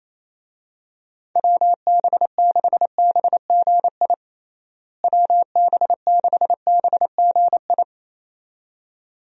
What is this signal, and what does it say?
Morse code, heard as a keyed tone, about 28 wpm, 700 Hz. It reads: WB6BGI WB6BGI
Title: USA Call Signs - 6 Characters - Repeated 2x - 28wpm